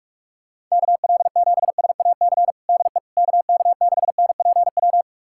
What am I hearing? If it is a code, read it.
Morse code, heard as a keyed tone, about 46 wpm, 700 Hz. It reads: KL7SAC DE KK6NPW
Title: USA Callsign DE Callsign - 6 Characters - 45wpm